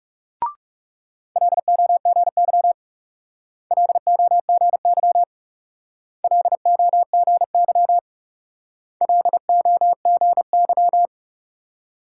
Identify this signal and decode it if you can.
Morse code, heard as a keyed tone, about 39 wpm, 700 Hz. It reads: LOGY LOGY LOGY LOGY